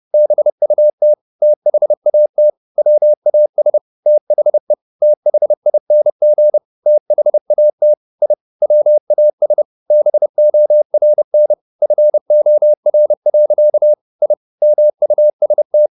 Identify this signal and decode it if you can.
Morse code, heard as a keyed tone, about 30 wpm, 600 Hz. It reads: BUT THAT WAS THE THING THAT I WAS BORN FOR. I MUST
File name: the old man and the sea / the old man and the sea 1 30wpm